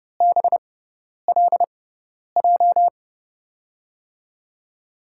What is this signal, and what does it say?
Morse code, heard as a keyed tone, about 31 wpm, spaced as for 10 wpm, 700 Hz. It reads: BLJ
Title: USA Call Signs - 6x Character Spacing - 30wpm